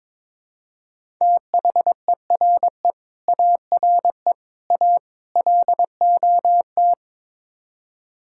Morse code, heard as a keyed tone, about 22 wpm, 700 Hz. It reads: THERE ARE A LOT
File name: practice / 22wpm / Build A Phrase - 5-Word Phrases - 22wpm